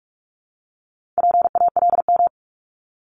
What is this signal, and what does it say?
Morse code, heard as a keyed tone, about 46 wpm, 700 Hz. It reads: PALM
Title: Multi-letter Phonemes Repeated 3x 45wpm